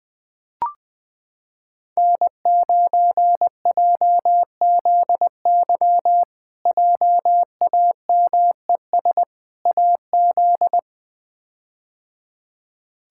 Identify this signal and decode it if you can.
Morse code, heard as a keyed tone, about 20 wpm, 700 Hz. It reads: N9JZY JAMES AZ
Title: K1USN SST QSO Practice - 20wpm